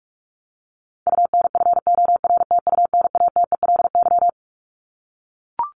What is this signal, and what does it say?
Morse code, heard as a keyed tone, about 44 wpm, 700 Hz. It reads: UNFORTUNATELY
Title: LY Suffix - Top 100 Words with Base Words 45wpm